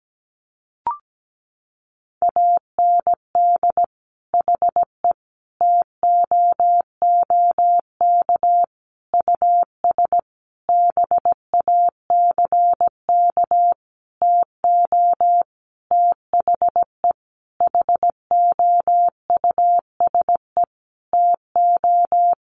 Morse code, heard as a keyed tone, about 17 wpm, 700 Hz. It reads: AND HE TOOK US BACK TO THE HOUSE TO
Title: Sentences from Top 1500 Words 17wpm